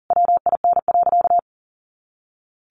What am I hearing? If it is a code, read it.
Morse code, heard as a keyed tone, about 40 wpm, 700 Hz. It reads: WIN.
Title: Sentences from Top 1500 Words 40wpm